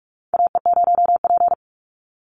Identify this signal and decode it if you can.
Morse code, heard as a keyed tone, about 44 wpm, 700 Hz. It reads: AE0P
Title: Call Signs Repeated 3x - 45wpm